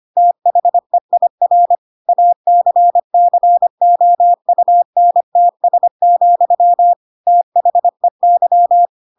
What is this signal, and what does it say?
Morse code, heard as a keyed tone, about 25 wpm, 700 Hz. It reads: THEIR ACCOUNTS, THEY
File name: Pinocchio / Pinocchio ch2 25 25 ik1hge